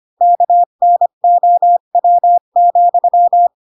Morse code, heard as a keyed tone, about 25 wpm, 700 Hz. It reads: KNOW,
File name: Pinocchio / Pinocchio ch7 20 25 ik1hge